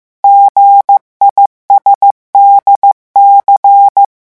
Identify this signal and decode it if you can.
Morse code, heard as a keyed tone, about 15 wpm, 800 Hz. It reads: GISDC
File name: morse mp3 / revision7-vitesse 14 mots